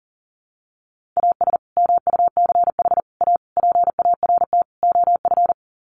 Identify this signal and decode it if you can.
Morse code, heard as a keyed tone, about 40 wpm, 700 Hz. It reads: AS MUCH A PART OF
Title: Sets of 5 Words Encore2 40wpm